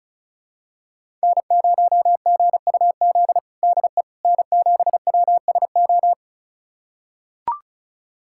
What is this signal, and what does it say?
Morse code, heard as a keyed tone, about 36 wpm, 700 Hz. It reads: N0GUZ DE N7WSO
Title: USA Callsign DE Callsign - 5 Characters - 35wpm